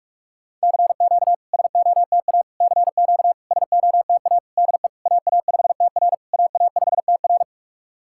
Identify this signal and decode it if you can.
Morse code, heard as a keyed tone, about 42 wpm, 700 Hz. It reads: CQ SOTA CQ SOTA DE AA5TR AA5TR
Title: SOTA QSO Practice 45wpm